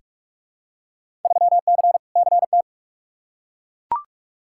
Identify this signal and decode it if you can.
Morse code, heard as a keyed tone, about 42 wpm, 700 Hz. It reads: 3C CT
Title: Field Day Exchanges Encore - 45wpm